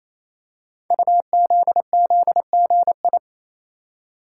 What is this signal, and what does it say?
Morse code, heard as a keyed tone, about 28 wpm, 700 Hz. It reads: UZZGI
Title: Code Groups - 5 Letters - 28wpm